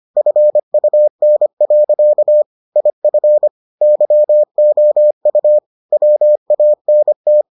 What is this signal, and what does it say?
Morse code, heard as a keyed tone, about 25 wpm, 600 Hz. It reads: FUN. IF YOU WANT